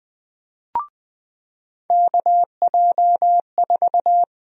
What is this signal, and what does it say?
Morse code, heard as a keyed tone, about 20 wpm, 700 Hz. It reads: KJ4
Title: USA Call Sign Prefixes - 20wpm